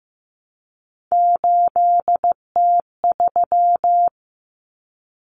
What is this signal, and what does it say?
Morse code, heard as a keyed tone, about 15 wpm, 700 Hz. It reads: 8T3
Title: Common Cut Numbers - 3-Digit 15wpm